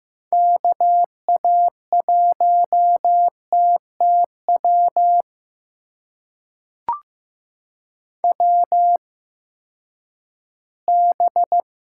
Morse code, heard as a keyed tone, about 15 wpm, spaced as for 10 wpm, 700 Hz. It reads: KA1TTW WB
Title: USA Call Signs - 6 Characters - 8x Character Spacing - 15wpm